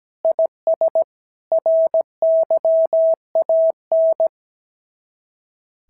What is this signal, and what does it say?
Morse code, heard as a keyed tone, about 16 wpm, 650 Hz. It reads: IS RYAN
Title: SOTA QSO Practice 17wpm